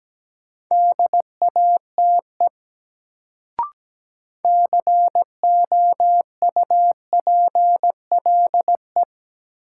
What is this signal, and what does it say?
Morse code, heard as a keyed tone, about 17 wpm, 700 Hz. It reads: DATE COUPLE
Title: Sentences from Top 1100 Words - New Word Drill Alt - 17wpm